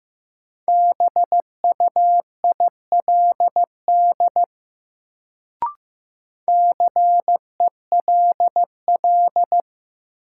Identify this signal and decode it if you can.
Morse code, heard as a keyed tone, about 15 wpm, 700 Hz. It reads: BUILD CELL
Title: Sentences from Top 1100 Words - New Word Drill Alt - 15wpm